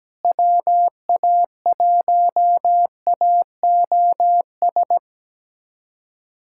Morse code, heard as a keyed tone, about 17 wpm, 700 Hz. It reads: WA1AOS
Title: USA Callsign DE Callsign - 6 Characters - 17wpm